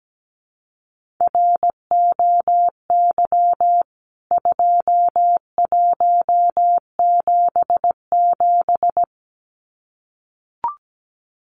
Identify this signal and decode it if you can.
Morse code, heard as a keyed tone, about 17 wpm, 700 Hz. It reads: ROY 2177
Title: CWT Exchange 17wpm